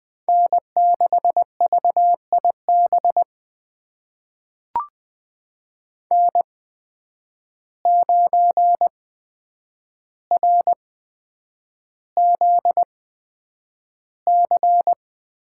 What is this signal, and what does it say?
Morse code, heard as a keyed tone, about 20 wpm, spaced as for 9 wpm, 700 Hz. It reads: N6VIB N9RZC
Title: USA Call Signs - 8x Character Spacing - 20wpm